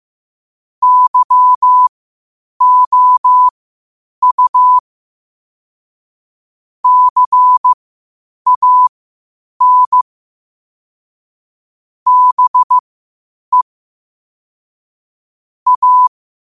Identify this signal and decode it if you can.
Morse code, heard as a keyed tone, about 15 wpm, spaced as for 8 wpm, 1000 Hz. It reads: YOU CAN BE A
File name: qotdfiles / 8wpm